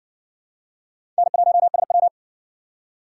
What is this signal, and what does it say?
Morse code, heard as a keyed tone, about 60 wpm, 700 Hz. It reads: N1SW